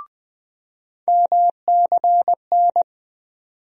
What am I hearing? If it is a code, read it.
Morse code, heard as a keyed tone, about 20 wpm, 700 Hz. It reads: MCN
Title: USA Call Sign Postfixes - 3 Letters 20wpm